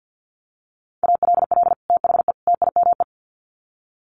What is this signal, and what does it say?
Morse code, heard as a keyed tone, about 50 wpm, 700 Hz. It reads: ALL THE TIME
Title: Sets of 3 Words 50wpm